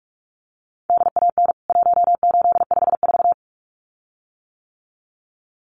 Morse code, heard as a keyed tone, about 44 wpm, 700 Hz. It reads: DAN 1854
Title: CWT Exchange 45wpm